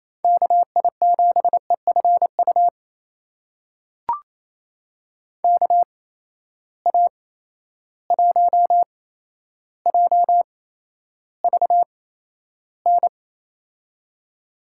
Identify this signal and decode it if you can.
Morse code, heard as a keyed tone, about 28 wpm, spaced as for 12 wpm, 700 Hz. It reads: KI7EFU KA1JVN
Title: USA Call Signs - 6 Characters - 8x Character Spacing - 28wpm